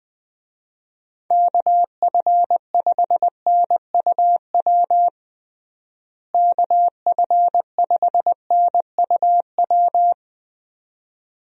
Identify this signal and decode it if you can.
Morse code, heard as a keyed tone, about 20 wpm, 700 Hz. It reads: KF5NUW KF5NUW
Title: USA Call Signs - 6 Characters - Repeated 2x - 20wpm